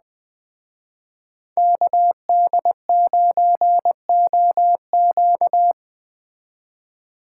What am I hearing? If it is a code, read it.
Morse code, heard as a keyed tone, about 20 wpm, 700 Hz. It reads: KD9OQ